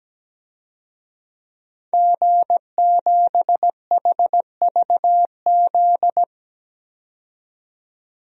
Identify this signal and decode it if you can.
Morse code, heard as a keyed tone, about 17 wpm, 700 Hz. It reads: G7HVZ
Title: DXCC-100 Call Signs - 5 Characters - 17wpm